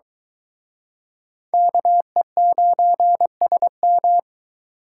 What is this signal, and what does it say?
Morse code, heard as a keyed tone, about 22 wpm, 700 Hz. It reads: KE9SM